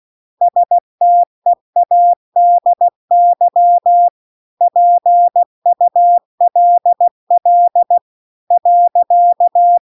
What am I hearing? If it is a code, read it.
Morse code, heard as a keyed tone, about 16 wpm, 700 Hz. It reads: STEADY PULL .